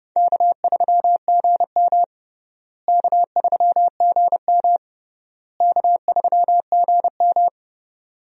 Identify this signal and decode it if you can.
Morse code, heard as a keyed tone, about 30 wpm, 700 Hz. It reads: K3GM K3GM K3GM
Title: USA Call Signs - 4 Characters - Repeated 3x - 30wpm